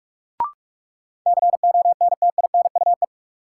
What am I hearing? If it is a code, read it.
Morse code, heard as a keyed tone, about 47 wpm, 700 Hz. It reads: CONTINUE
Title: Sentences from Top 1100 Words - New Word Drill Alt - 45wpm